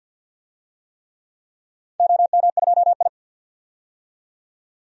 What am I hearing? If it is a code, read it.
Morse code, heard as a keyed tone, about 48 wpm, 700 Hz. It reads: OM2I
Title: DXCC-100 Call Signs - 4 Characters - 50wpm